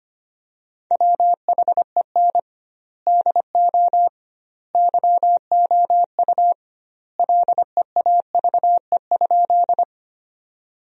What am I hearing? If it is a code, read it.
Morse code, heard as a keyed tone, about 25 wpm, 700 Hz. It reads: WHEN DO YOU LEAVE?
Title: 2x Word-Spacing - Sentences from Top 400 Words - 25wpm